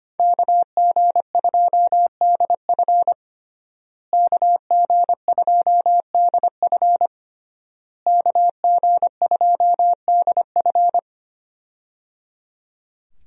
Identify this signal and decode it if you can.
Morse code, heard as a keyed tone, about 25 wpm, 700 Hz. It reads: KG2DF KG2DF KG2DF
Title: USA Call Signs - 5 Characters - Repeated 3x - 25wpm